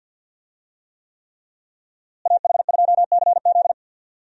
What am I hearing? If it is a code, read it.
Morse code, heard as a keyed tone, about 46 wpm, 700 Hz. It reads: AH2CZ
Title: USA Call Signs - 5 Characters - Repeated 3x - 50wpm